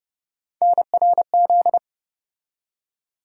Morse code, heard as a keyed tone, about 32 wpm, 700 Hz. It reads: NRZ
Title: USA Call Sign Postfixes - 3 Letters 30wpm